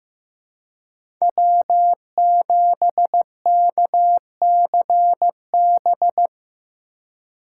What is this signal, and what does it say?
Morse code, heard as a keyed tone, about 15 wpm, 700 Hz. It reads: W7KCB
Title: USA Call Signs - 5 Characters - 3x Character Spacing - 15wpm